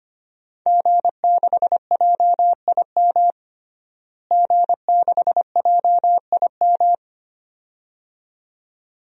Morse code, heard as a keyed tone, about 25 wpm, 700 Hz. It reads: G6JIM G6JIM
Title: DXCC-100 Call Signs - 5 Characters - Repeated 2x - 25wpm